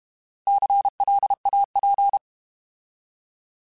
Morse code, heard as a keyed tone, about 32 wpm, 800 Hz. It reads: CLAP
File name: Alphabetical / Alphabetical mp3s / Cs @30